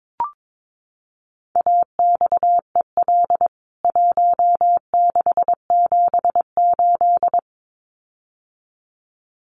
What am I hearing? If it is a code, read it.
Morse code, heard as a keyed tone, about 22 wpm, 700 Hz. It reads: AXEL 1678
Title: CWT Exchange 22wpm